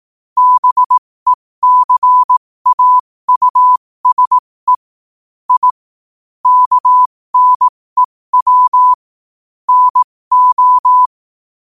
Morse code, heard as a keyed tone, about 18 wpm, spaced as for 15 wpm, 1000 Hz. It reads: BECAUSE I KNEW NO
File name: qotdfiles / 15wpm